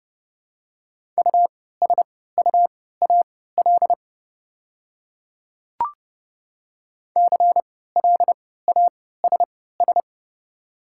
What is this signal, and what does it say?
Morse code, heard as a keyed tone, about 30 wpm, spaced as for 17 wpm, 690 Hz. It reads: USUAL CLASS
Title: 3x Character Spacing - Top 500 Words - 30wpm